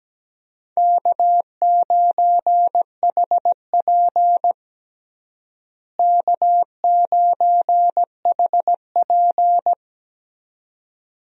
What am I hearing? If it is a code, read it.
Morse code, heard as a keyed tone, about 17 wpm, 700 Hz. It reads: K9HP K9HP
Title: USA Call Signs - 4 Characters - Repeated 2x - 17wpm